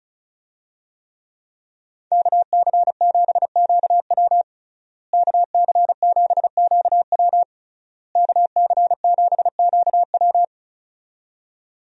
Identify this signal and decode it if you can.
Morse code, heard as a keyed tone, about 36 wpm, 700 Hz. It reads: KC7QW KC7QW KC7QW
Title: USA Call Signs - 5 Characters - Repeated 3x - 35wpm